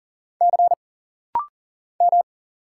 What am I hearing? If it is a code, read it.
Morse code, heard as a keyed tone, about 40 wpm, 700 Hz. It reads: C M
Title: Single Letters - Rapid-Fire 40wpm